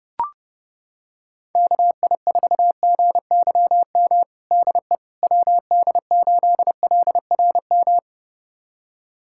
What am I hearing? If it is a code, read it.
Morse code, heard as a keyed tone, about 30 wpm, 700 Hz. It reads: KI4GYM DE WD8LRM
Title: USA Callsign DE Callsign - 6 Characters - 30wpm